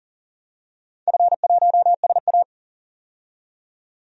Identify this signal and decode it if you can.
Morse code, heard as a keyed tone, about 40 wpm, 700 Hz. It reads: F1SA
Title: DXCC-100 Call Signs - 4 Characters - 40wpm